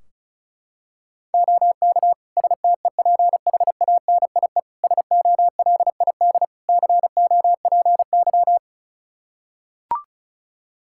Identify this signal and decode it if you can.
Morse code, heard as a keyed tone, about 36 wpm, 700 Hz. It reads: OK STEPHANIE SOLID COPY
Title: Practice QSO - Exchange 3 - More Info - 35wpm